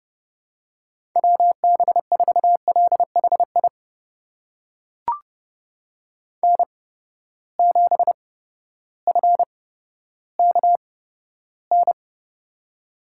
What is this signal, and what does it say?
Morse code, heard as a keyed tone, about 30 wpm, spaced as for 14 wpm, 700 Hz. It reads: WB4LHI N7FKN
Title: USA Call Signs - 8x Character Spacing - 30wpm